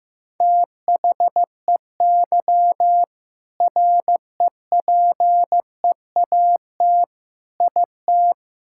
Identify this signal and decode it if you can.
Morse code, heard as a keyed tone, about 15 wpm, 700 Hz. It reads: THEY REPEAT IT
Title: Sentences from Top 1600 Words 15wpm